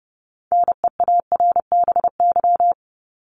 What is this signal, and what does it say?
Morse code, heard as a keyed tone, about 30 wpm, 700 Hz. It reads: NEARBY
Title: Top 1501-1750 Words 30wpm